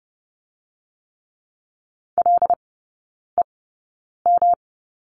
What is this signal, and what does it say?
Morse code, heard as a keyed tone, about 30 wpm, spaced as for 9 wpm, 700 Hz. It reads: LEM
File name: Lesson 19e - Letter w - Three Characters 30wpm